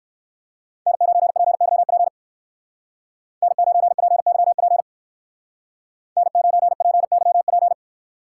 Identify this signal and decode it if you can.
Morse code, heard as a keyed tone, about 59 wpm, 700 Hz. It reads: N9PYP N9PYP N9PYP